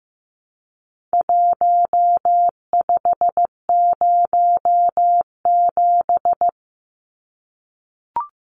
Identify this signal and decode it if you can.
Morse code, heard as a keyed tone, about 15 wpm, 700 Hz. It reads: 1507
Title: Numbers - Contest Style - 4-Digit 15wpm